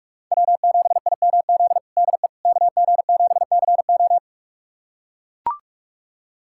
Morse code, heard as a keyed tone, about 45 wpm, 700 Hz. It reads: W7IMZ DE KG7CO
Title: USA Callsign DE Callsign - 5 Characters - 45wpm